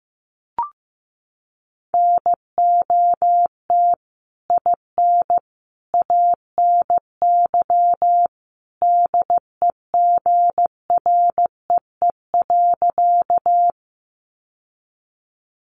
Morse code, heard as a keyed tone, about 15 wpm, 700 Hz. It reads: NOT IN ANY DEGREE.
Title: Sentences from Top 1500 Words 15wpm